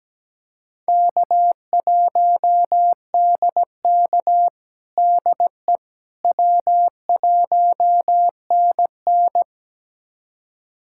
Morse code, heard as a keyed tone, about 17 wpm, 700 Hz. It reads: K1DK DE W1NN
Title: USA Callsign DE Callsign - 4 Characters - 17wpm